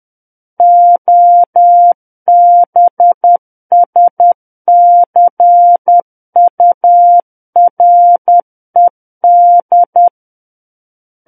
Morse code, heard as a keyed tone, about 10 wpm, 700 Hz. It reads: OBSCURED
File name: Book One / 01-Chapter 1 - 10 WPM